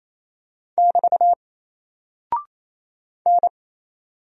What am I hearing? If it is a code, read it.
Morse code, heard as a keyed tone, about 27 wpm, 700 Hz. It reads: <BT> N
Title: Single Character (Letter:Number:Punctuation:Prosign) 28wpm